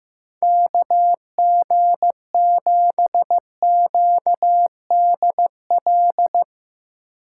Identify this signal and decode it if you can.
Morse code, heard as a keyed tone, about 15 wpm, 700 Hz. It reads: KG7QDL
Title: USA Call Signs - 6 Characters - 6x Character Spacing - 15wpm